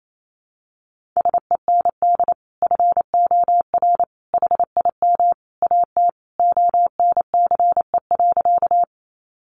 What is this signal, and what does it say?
Morse code, heard as a keyed tone, about 28 wpm, 700 Hz. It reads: SEND FOR HIM AT ONCE.
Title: Sentences from Top 1500 Words 28wpm